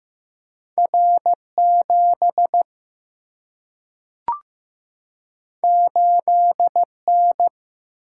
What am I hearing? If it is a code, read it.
Morse code, heard as a keyed tone, about 15 wpm, 700 Hz. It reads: R 7 8 N
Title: Single-Letter Single-Number 15wpm